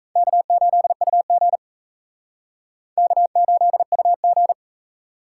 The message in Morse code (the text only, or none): K8UG K8UG